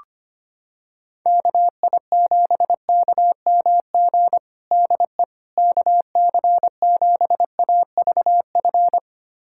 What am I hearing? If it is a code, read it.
Morse code, heard as a keyed tone, about 25 wpm, 700 Hz. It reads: KI7KMG DE KC7AVF
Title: USA Callsign DE Callsign - 6 Characters - 25wpm